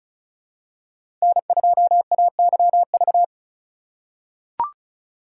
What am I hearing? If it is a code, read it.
Morse code, heard as a keyed tone, about 36 wpm, 700 Hz. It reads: N2AYV
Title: USA Call Signs - 5 Characters - 2x Character Spacing - 35wpm